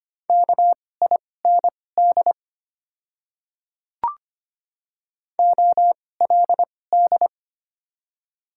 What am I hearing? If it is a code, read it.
Morse code, heard as a keyed tone, about 25 wpm, spaced as for 18 wpm, 700 Hz. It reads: KIND OLD
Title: 2x Character Spacing - Top 200 Words - 25wpm